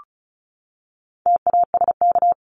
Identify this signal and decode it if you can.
Morse code, heard as a keyed tone, about 34 wpm, 700 Hz. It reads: TASK
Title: Top 751-1000 Words 35wpm